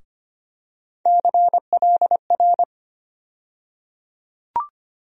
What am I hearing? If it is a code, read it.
Morse code, heard as a keyed tone, about 25 wpm, 700 Hz. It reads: CLR
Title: USA Call Sign Postfixes - 3 Letters 25wpm